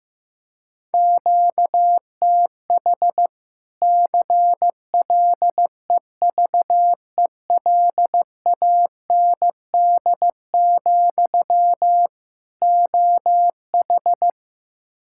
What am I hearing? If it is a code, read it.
Morse code, heard as a keyed tone, about 15 wpm, 700 Hz. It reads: QTH CLEVELAND, OH
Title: Practice QSO - Exchange 1 - Beginning - 15wpm